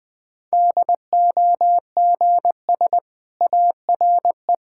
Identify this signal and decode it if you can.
Morse code, heard as a keyed tone, about 19 wpm, 700 Hz. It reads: DOGS ARE
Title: Difficult Sentences with Hints - 20wpm